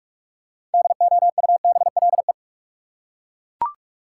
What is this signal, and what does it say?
Morse code, heard as a keyed tone, about 41 wpm, 700 Hz. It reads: DOUBLE
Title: Six Letter Words - 45wpm